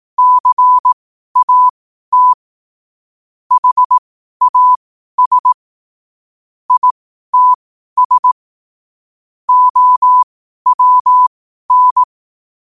Morse code, heard as a keyed tone, about 18 wpm, spaced as for 12 wpm, 1000 Hz. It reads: CAT HAS ITS OWN
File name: qotdfiles / 12wpm